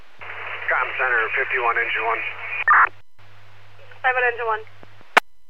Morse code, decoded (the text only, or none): none